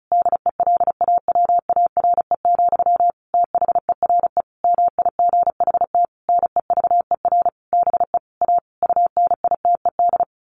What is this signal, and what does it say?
Morse code, heard as a keyed tone, about 34 wpm, 700 Hz. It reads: DELAWARE, THERE MIGHT NEVER BE A UNITED
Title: A Sound Of Thunder 35wpm